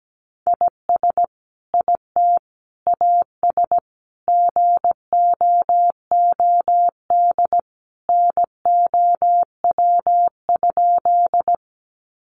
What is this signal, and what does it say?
Morse code, heard as a keyed tone, about 17 wpm, 700 Hz. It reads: IS IT AS GOOD NOW?
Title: Sentences from Top 200 Words 17wpm